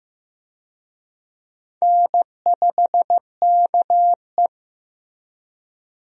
Morse code, heard as a keyed tone, about 15 wpm, 700 Hz. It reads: N5KE
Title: USA Call Signs - 4 Characters - Repeated 2x - 15wpm